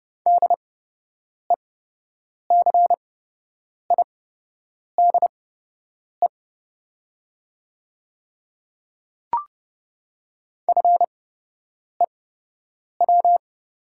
Morse code, heard as a keyed tone, about 31 wpm, spaced as for 8 wpm, 700 Hz. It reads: DECIDE FEW
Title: 8x Character Spacing - Top 500 Words - 30wpm